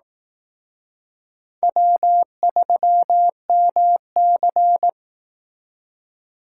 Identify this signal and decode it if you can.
Morse code, heard as a keyed tone, about 18 wpm, 700 Hz. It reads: W3MC
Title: Speed Racing - USA 1x2 Call Signs 15wpm